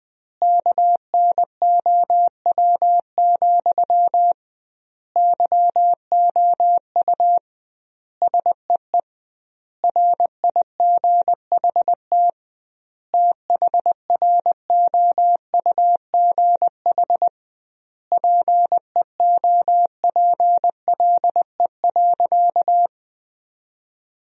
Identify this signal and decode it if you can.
Morse code, heard as a keyed tone, about 20 wpm, 700 Hz. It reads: KNOW, YOU SEE RIGHT THROUGH PEOPLE.
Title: 2x Word-Spacing - Sentences from Top 200 Words - 20wpm